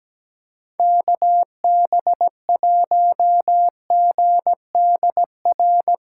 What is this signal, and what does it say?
Morse code, heard as a keyed tone, about 17 wpm, 700 Hz. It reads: KB1GDR
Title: USA Call Signs - 6 Characters - 3x Character Spacing - 17wpm